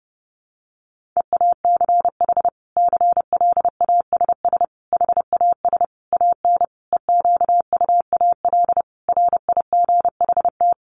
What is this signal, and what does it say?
Morse code, heard as a keyed tone, about 30 wpm, 700 Hz. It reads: EACH CLASS HAS AN EQUAL RIGHT
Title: Sentences from Top 1400 Words 30wpm